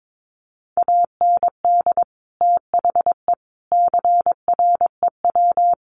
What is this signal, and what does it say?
Morse code, heard as a keyed tone, about 22 wpm, 700 Hz. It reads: AND THE CREW
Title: Sentences from Top 1500 Words 22wpm